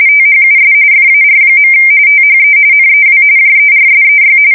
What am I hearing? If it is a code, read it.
RTTY, about 45.45 baud. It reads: THUS FAR. THE NEW JERSEY
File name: fsk170 / USS-New-Jersey